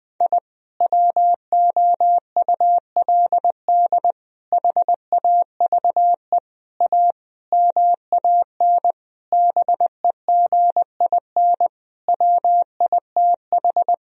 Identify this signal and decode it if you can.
Morse code, heard as a keyed tone, about 20 wpm, 700 Hz. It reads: I WOULD HAVE A MAN BEGIN WITH
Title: Sentences from Top 1000 Words - Review Part 2 - 20wpm